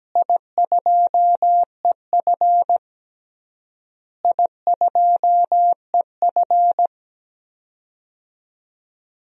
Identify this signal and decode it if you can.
Morse code, heard as a keyed tone, about 17 wpm, 700 Hz. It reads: I2EF I2EF
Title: DXCC-100 Call Signs - 4 Characters - Repeated 3x - 17wpm